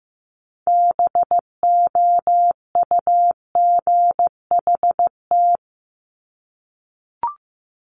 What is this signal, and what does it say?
Morse code, heard as a keyed tone, about 15 wpm, 700 Hz. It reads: BOUGHT
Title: Multi-letter Phonemes Repeated 3x 15wpm